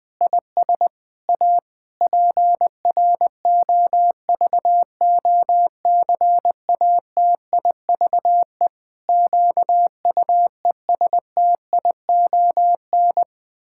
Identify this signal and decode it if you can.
Morse code, heard as a keyed tone, about 20 wpm, 700 Hz. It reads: IS A PROVOCATIVE QUESTION
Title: Pangram Challenge v2 - 20wpm